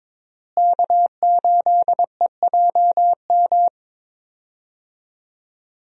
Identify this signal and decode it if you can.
Morse code, heard as a keyed tone, about 22 wpm, 700 Hz. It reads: K8EJM
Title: USA Call Signs - 5 Characters - Repeated 2x - 22wpm